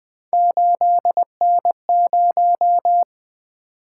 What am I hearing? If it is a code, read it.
Morse code, heard as a keyed tone, about 21 wpm, 700 Hz. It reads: 8N0
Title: Three Letter-Number Combinations - 20wpm